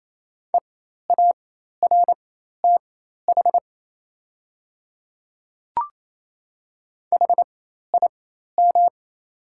Morse code, heard as a keyed tone, about 27 wpm, spaced as for 13 wpm, 700 Hz. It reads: EARTH HIM
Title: 4x Character Spacing - Top 300 Words - 28wpm